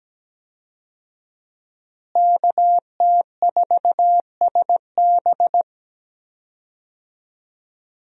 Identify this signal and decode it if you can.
Morse code, heard as a keyed tone, about 17 wpm, 700 Hz. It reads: KT4SB
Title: USA Call Signs - 5 Characters - Repeated 3x - 17wpm